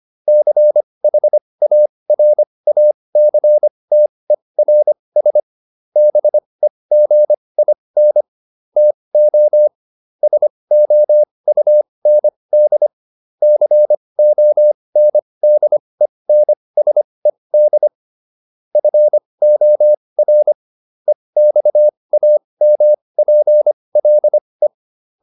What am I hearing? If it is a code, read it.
Morse code, heard as a keyed tone, about 25 wpm, spaced as for 20 wpm, 600 Hz. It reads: CHARACTERS BEGIN TO SOUND CONDENSED FOR EXAMPLE